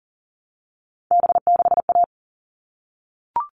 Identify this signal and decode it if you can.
Morse code, heard as a keyed tone, about 40 wpm, 700 Hz. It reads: B6A
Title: All Cut Numbers - 3-Digit 40wpm